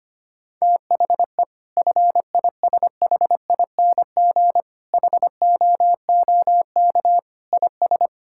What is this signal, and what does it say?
Morse code, heard as a keyed tone, about 25 wpm, 700 Hz. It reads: THE FISHING HOOK IS
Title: Sentences from Top 1600 Words 25wpm